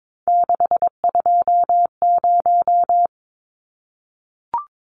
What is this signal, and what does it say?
Morse code, heard as a keyed tone, about 22 wpm, 700 Hz. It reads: 620
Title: Numbers - 3-Digit 22wpm